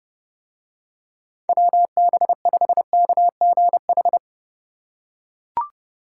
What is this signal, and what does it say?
Morse code, heard as a keyed tone, about 31 wpm, 700 Hz. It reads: WB5KGH